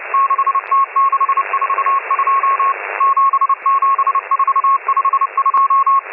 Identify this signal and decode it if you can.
Morse code, heard as a keyed tone, about 30 wpm, 1090 Hz, with radio noise. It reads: 6T642 77452